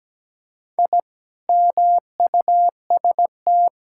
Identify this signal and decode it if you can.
Morse code, heard as a keyed tone, about 17 wpm, 700 Hz. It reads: I MUST
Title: Sentences from Top 1000 Words - Review Part 6 - 17wpm